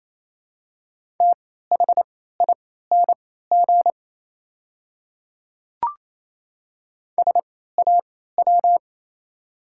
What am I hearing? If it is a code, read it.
Morse code, heard as a keyed tone, about 28 wpm, spaced as for 16 wpm, 700 Hz. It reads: THING SAW
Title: 3x Character Spacing - Top 300 Words - 28wpm